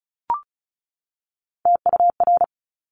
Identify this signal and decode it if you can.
Morse code, heard as a keyed tone, about 36 wpm, 700 Hz. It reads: TUR
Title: Top Three Letter Combinations in Words 35wpm